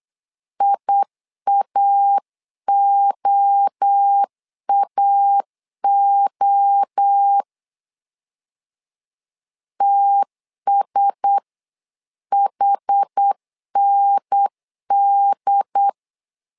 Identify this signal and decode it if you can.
Morse code, heard as a keyed tone, about 8 wpm, 800 Hz. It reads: IAOAO TSHND
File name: LEZIONI / 4 Q1S 8wpm 2710